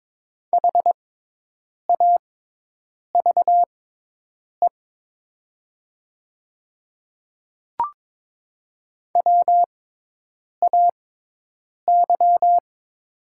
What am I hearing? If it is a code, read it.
Morse code, heard as a keyed tone, about 22 wpm, spaced as for 8 wpm, 700 Hz. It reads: HAVE WAY
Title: 6x Character Spacing - Top 100 Words - 22wpm